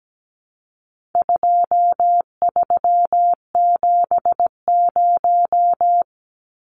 Morse code, heard as a keyed tone, about 17 wpm, 700 Hz. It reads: 2370
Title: Numbers - 4-Digit 17wpm